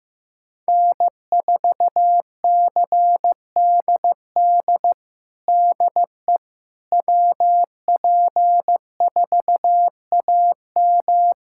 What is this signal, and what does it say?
Morse code, heard as a keyed tone, about 15 wpm, 700 Hz. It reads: N4CDD DE WP4AM
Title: USA Callsign DE Callsign - 5 Characters - 15wpm